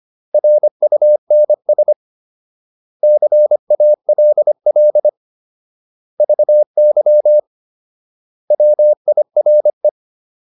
Morse code, heard as a keyed tone, about 25 wpm, 600 Hz. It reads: RUNS CALL VY WIRE